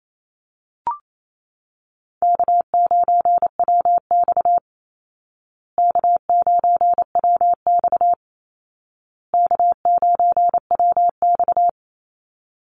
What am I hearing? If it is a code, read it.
Morse code, heard as a keyed tone, about 28 wpm, 700 Hz. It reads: K9WX K9WX K9WX
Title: Easy Call Signs Repeated 3x - 28wpm